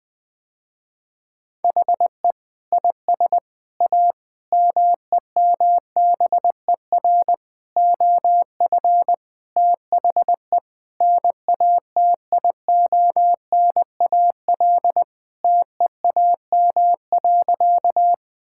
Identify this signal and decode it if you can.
Morse code, heard as a keyed tone, about 20 wpm, 700 Hz. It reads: HE IS A MEMBER OF THE NATIONAL TEAM.
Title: Sentences with 3-Word Phrases - Spoken Beginning - 20wpm